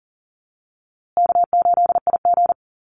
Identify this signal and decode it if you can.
Morse code, heard as a keyed tone, about 39 wpm, 700 Hz. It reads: K8IG
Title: USA Call Signs Repeated 3x 40wpm